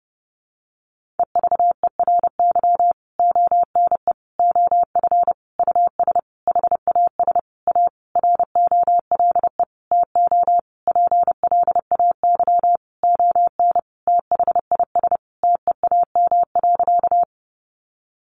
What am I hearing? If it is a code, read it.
Morse code, heard as a keyed tone, about 28 wpm, 700 Hz. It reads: EVERY ONE OF US HAS A ROLE TO PLAY ON THIS TEAM.
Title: Sentences from Top 1500 Words 30wpm